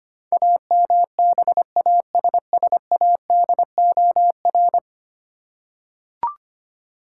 Morse code, heard as a keyed tone, about 25 wpm, 700 Hz. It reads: AMBASSADOR
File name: 10 Letter Words - First 3 Letters Given - 25wpm